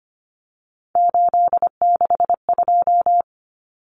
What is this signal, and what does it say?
Morse code, heard as a keyed tone, about 25 wpm, 700 Hz. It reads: 862
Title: Numbers - 3-Digit 25wpm